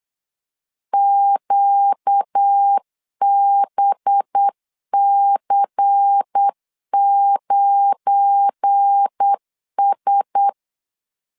Morse code, heard as a keyed tone, about 8 wpm, 800 Hz. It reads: QBC9S
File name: LEZIONI / 7 GF9Z / 7 GF9 8wpm 3545